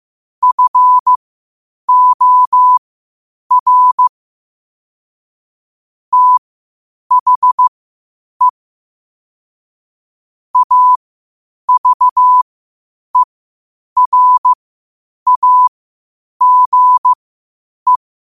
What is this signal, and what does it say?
Morse code, heard as a keyed tone, about 15 wpm, spaced as for 8 wpm, 1000 Hz. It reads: FOR THE AVERAGE
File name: qotdfiles / 8wpm